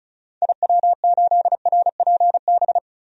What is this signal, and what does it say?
Morse code, heard as a keyed tone, about 36 wpm, 700 Hz. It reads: IW8RPB
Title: DXCC-100 Call Signs - 6 Characters - 35wpm